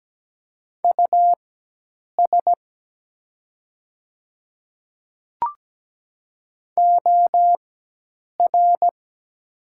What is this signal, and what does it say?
Morse code, heard as a keyed tone, about 17 wpm, spaced as for 8 wpm, 700 Hz. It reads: US OR
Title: 4x Character Spacing - Top 200 Words - 17wpm